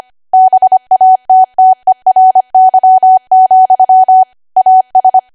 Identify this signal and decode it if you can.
Morse code, heard as a keyed tone, about 25 wpm, 750 Hz. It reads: BATTERY, AS
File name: Archive / 25 WPM / 25WPM